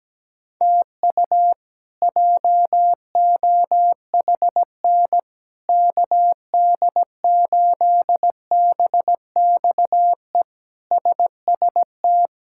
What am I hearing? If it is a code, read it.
Morse code, heard as a keyed tone, about 17 wpm, 700 Hz. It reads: TU JOHN KD8BXE SST
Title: K1USN SST QSO Practice - 17wpm